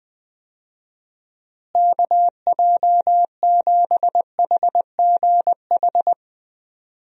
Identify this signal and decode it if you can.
Morse code, heard as a keyed tone, about 21 wpm, 700 Hz. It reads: KJ7HGH
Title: USA Call Signs - 6 Characters - Repeated 2x - 20wpm